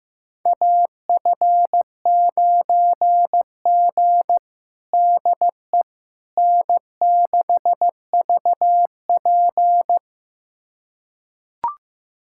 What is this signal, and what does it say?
Morse code, heard as a keyed tone, about 15 wpm, 700 Hz. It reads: AF9G DE N6VP
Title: USA Callsign DE Callsign - 4 Characters - 15wpm